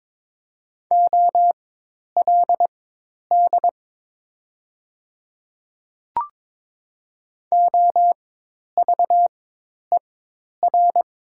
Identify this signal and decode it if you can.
Morse code, heard as a keyed tone, about 22 wpm, spaced as for 10 wpm, 700 Hz. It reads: OLD OVER
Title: 4x Character Spacing - Top 500 Words - 22wpm